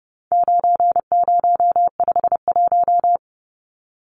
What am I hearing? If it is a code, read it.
Morse code, heard as a keyed tone, about 30 wpm, 700 Hz. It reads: 9051
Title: Numbers - 4-Digit 30wpm